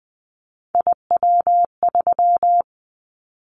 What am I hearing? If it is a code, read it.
Morse code, heard as a keyed tone, about 20 wpm, 700 Hz. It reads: IW3
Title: Callsign Prefixes for Top 25 DXCC - Repeated 3x - 20wpm